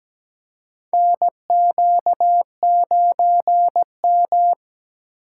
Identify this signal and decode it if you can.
Morse code, heard as a keyed tone, about 17 wpm, 700 Hz. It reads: NQ9M
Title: USA Call Signs - 2x Character Spacing - 17wpm